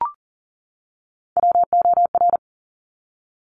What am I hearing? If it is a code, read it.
Morse code, heard as a keyed tone, about 38 wpm, 700 Hz. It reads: WOR
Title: Top Three Letter Combinations in Words 40wpm